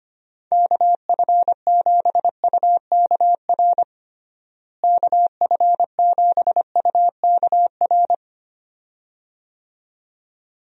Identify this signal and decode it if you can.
Morse code, heard as a keyed tone, about 25 wpm, 700 Hz. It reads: KF7UKR KF7UKR
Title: USA Call Signs - 6 Characters - Repeated 2x - 25wpm